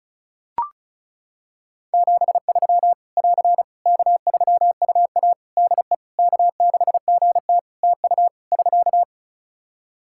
Morse code, heard as a keyed tone, about 36 wpm, 700 Hz. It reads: 73 <AR> K3UA DE K6GT TU <SK>
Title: Practice QSO - Exchange 4 - Ending - 35wpm